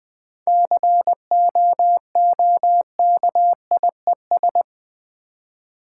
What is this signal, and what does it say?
Morse code, heard as a keyed tone, about 20 wpm, 700 Hz. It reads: COOKIES
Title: Easy Words with Hints - 20wpm